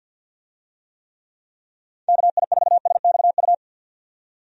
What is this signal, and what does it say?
Morse code, heard as a keyed tone, about 51 wpm, 700 Hz. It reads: KI4SXU
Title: USA Call Signs - 6 Characters - 50wpm